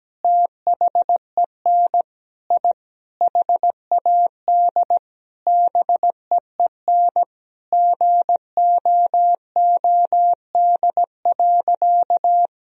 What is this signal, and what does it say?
Morse code, heard as a keyed tone, about 17 wpm, 700 Hz. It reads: THEN I HAD BEEN GOOD.
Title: Sentences from Top 1000 Words - Review Part 1 - 17wpm